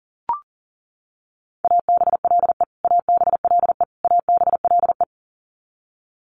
Morse code, heard as a keyed tone, about 40 wpm, 700 Hz. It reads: ABLE ABLE ABLE
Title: Top 500 Words Repeated 40wpm